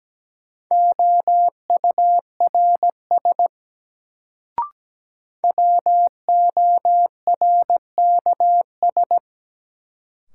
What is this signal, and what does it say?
Morse code, heard as a keyed tone, about 17 wpm, 700 Hz. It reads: OURS WORKS
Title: Sentences from Top 1000 Words - Review Part 5 - New Word Drill - 17wpm